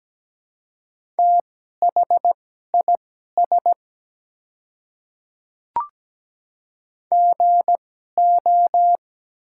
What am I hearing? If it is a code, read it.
Morse code, heard as a keyed tone, about 17 wpm, spaced as for 12 wpm, 700 Hz. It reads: THIS GO
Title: 2x Character Spacing - Top 100 Words - 17wpm